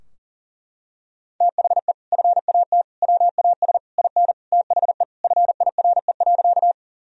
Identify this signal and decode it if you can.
Morse code, heard as a keyed tone, about 40 wpm, 700 Hz. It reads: THE FAT WAS IN THE FIRE.
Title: Sentences from Top 1000 Words - Review Part 4 - 40wpm